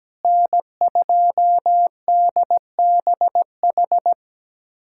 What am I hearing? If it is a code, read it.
Morse code, heard as a keyed tone, about 17 wpm, 700 Hz. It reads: N2DBH
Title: USA Callsign DE Callsign - 5 Characters - 17wpm